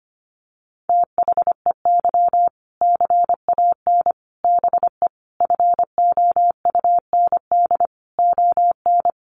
Morse code, heard as a keyed tone, about 25 wpm, 700 Hz. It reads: THEY CAN BE FOUND ON
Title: Sentences from Top 1500 Words 25wpm